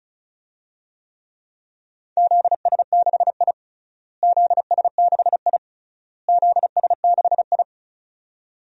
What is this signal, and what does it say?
Morse code, heard as a keyed tone, about 34 wpm, 700 Hz. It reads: ZS6I ZS6I ZS6I
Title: DXCC-100 Call Signs - 4 Characters - Repeated 3x - 35wpm